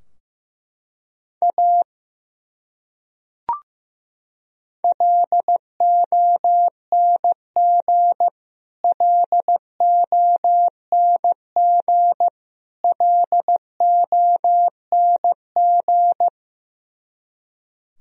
Morse code, heard as a keyed tone, about 16 wpm, 700 Hz. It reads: A LONG LONG LONG
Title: Top 100 Words Repeated 15wpm